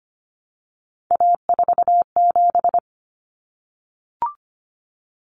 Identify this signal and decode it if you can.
Morse code, heard as a keyed tone, about 25 wpm, 700 Hz. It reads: A47
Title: Common Cut Numbers - 3-Digit 25wpm